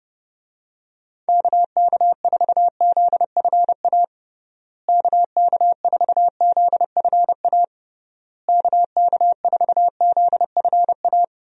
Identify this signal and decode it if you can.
Morse code, heard as a keyed tone, about 32 wpm, 700 Hz. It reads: KK4ZFA KK4ZFA KK4ZFA
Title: USA Call Signs - 6 Characters - Repeated 3x - 30wpm